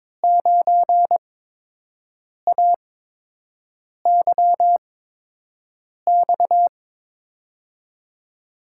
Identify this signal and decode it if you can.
Morse code, heard as a keyed tone, about 22 wpm, spaced as for 6 wpm, 700 Hz. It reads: 9AYX
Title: USA Call Signs - 6 Characters - 8x Character Spacing - 22wpm